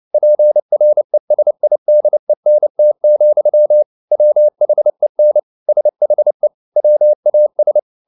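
Morse code, heard as a keyed tone, about 29 wpm, 600 Hz. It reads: PRESIDENT, WHEN SHE WAS
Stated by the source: Morse code at 29 wpm, sending Ken Follett's novel Never in English, from CW med Gnister.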